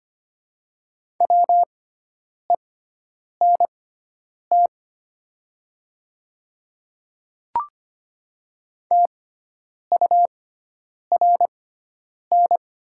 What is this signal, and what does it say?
Morse code, heard as a keyed tone, about 25 wpm, spaced as for 9 wpm, 690 Hz. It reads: WENT TURN
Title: 6x Character Spacing - Top 200 Words - 25wpm